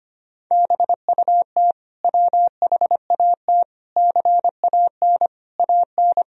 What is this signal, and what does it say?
Morse code, heard as a keyed tone, about 25 wpm, 700 Hz. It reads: BUT WHAT CAN AN
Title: Sentences from Top 1000 Words - Review Part 4 - 25wpm